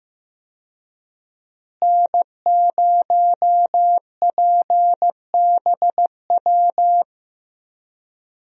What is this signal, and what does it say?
Morse code, heard as a keyed tone, about 15 wpm, 700 Hz. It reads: N0PBW
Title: USA Call Signs - 5 Characters - Repeated 2x - 15wpm